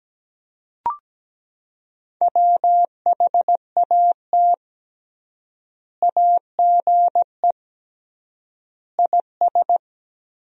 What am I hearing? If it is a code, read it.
Morse code, heard as a keyed tone, about 17 wpm, 700 Hz. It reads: WHAT AGE IS
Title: 3x Word-Spacing - Sentences from Top 500 Words - 17wpm